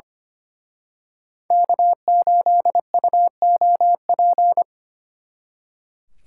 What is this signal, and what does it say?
Morse code, heard as a keyed tone, about 30 wpm, 700 Hz. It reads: K8UOP
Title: Speed Racing - USA 1x3 Call Signs 25wpm